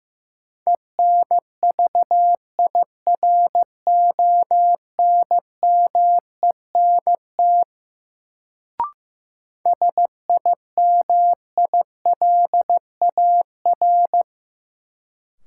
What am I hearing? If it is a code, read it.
Morse code, heard as a keyed tone, about 15 wpm, 700 Hz. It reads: ENVIRONMENT SIMILAR
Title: Sentences from Top 1000 Words - Review Part 5 - New Word Drill - 15wpm